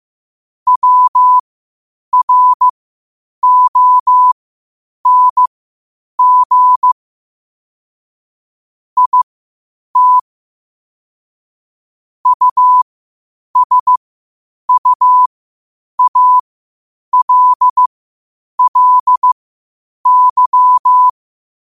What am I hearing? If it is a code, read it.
Morse code, heard as a keyed tone, about 15 wpm, spaced as for 8 wpm, 1000 Hz. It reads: WRONG IT USUALLY
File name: qotdfiles / 8wpm